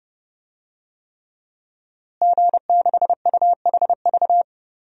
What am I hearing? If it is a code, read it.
Morse code, heard as a keyed tone, about 31 wpm, 700 Hz. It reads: G6UHV
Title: DXCC-100 Call Signs - 5 Characters - Repeated 3x - 30wpm